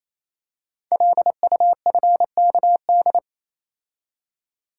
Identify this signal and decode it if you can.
Morse code, heard as a keyed tone, about 29 wpm, 700 Hz. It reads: LUFKD
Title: Code Groups - 5 Letters - 28wpm